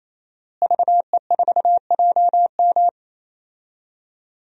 Morse code, heard as a keyed tone, about 28 wpm, 700 Hz. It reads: VE4JM
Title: DXCC-100 Call Signs - 5 Characters - Repeated 3x - 28wpm